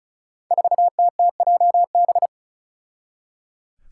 Morse code, heard as a keyed tone, about 34 wpm, 700 Hz. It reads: 4TTJB
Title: Five Challenging Letter-Number Combinations 35wpm